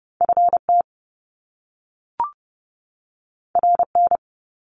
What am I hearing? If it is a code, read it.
Morse code, heard as a keyed tone, about 30 wpm, 700 Hz. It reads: FT RN
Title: Top Two Letter Combinations in Words 30wpm